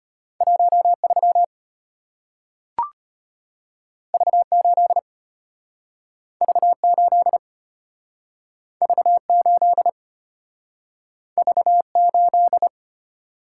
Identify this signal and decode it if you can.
Morse code, heard as a keyed tone, about 34 wpm, 700 Hz. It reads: 13 V8 V8 V8 V8